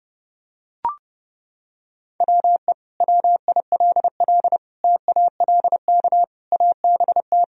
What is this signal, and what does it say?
Morse code, heard as a keyed tone, about 32 wpm, 700 Hz. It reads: WE WILL TALK ABT
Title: Sentences from Top 400 Words and Abbreviations 30wpm